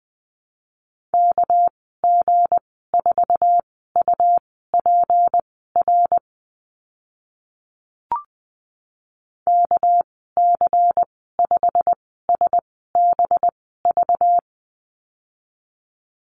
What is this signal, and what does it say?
Morse code, heard as a keyed tone, about 20 wpm, spaced as for 14 wpm, 700 Hz. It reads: KG4UPR KC5SBV
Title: 231 Lesson 37g - Letter x - Call Signs 20wpm